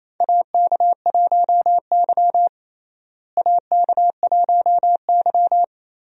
Morse code, heard as a keyed tone, about 28 wpm, 700 Hz. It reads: AK1Y AK1Y
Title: USA Call Signs - 4 Characters - Repeated 3x - 28wpm